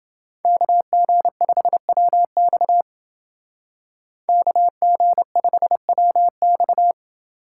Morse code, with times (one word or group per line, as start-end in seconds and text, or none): 0.5-2.8 KG5WX
4.3-6.9 KG5WX